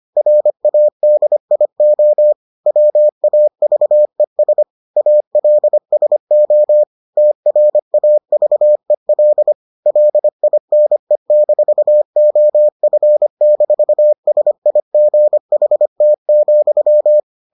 Morse code, heard as a keyed tone, about 25 wpm, 600 Hz. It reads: RADIO WAVES ALSO TRAVEL LINE-OF-SIGHT,